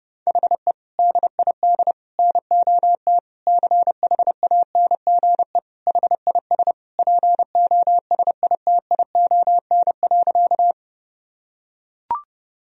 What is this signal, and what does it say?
Morse code, heard as a keyed tone, about 32 wpm, 700 Hz. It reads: HE DID NOT CHANGE HIS POSITION.
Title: Sentences from Top 1000 Words - Review Part 1 - 30wpm